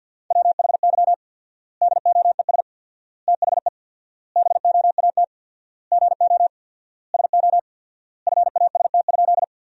Morse code, heard as a keyed tone, about 51 wpm, 700 Hz. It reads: WHY DOES THE BOAT GO SO FAST?
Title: 4x Word-Spacing - Sentences from Top 500 Words - 50wpm